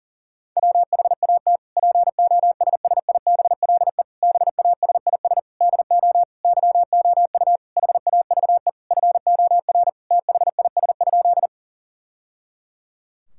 Morse code, heard as a keyed tone, about 37 wpm, 700 Hz. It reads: WHAT POSSIBLE BASIS DO YOU HAVE FOR THIS?
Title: Sentences from Top 1000 Words - Review Part 5 - 40wpm